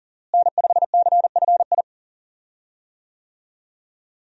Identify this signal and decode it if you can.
Morse code, heard as a keyed tone, about 40 wpm, 700 Hz. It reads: N5CFI
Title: USA Call Signs - 5 Characters - Repeated 3x - 40wpm